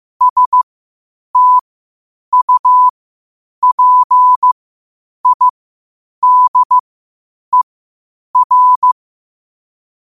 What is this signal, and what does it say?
Morse code, heard as a keyed tone, about 15 wpm, spaced as for 8 wpm, 1000 Hz. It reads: STUPIDER
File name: qotdfiles / 8wpm